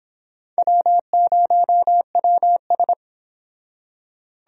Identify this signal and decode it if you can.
Morse code, heard as a keyed tone, about 25 wpm, 700 Hz. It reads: W0WS